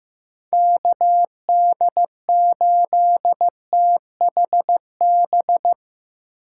Morse code, heard as a keyed tone, about 15 wpm, 700 Hz. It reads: KD8THB
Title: USA Call Signs - 6 Characters - 6x Character Spacing - 15wpm